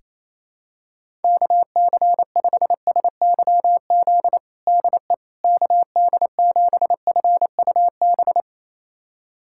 Morse code, heard as a keyed tone, about 28 wpm, 700 Hz. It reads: KC5SYZ DE KD7FUB
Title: USA Callsign DE Callsign - 6 Characters - 28wpm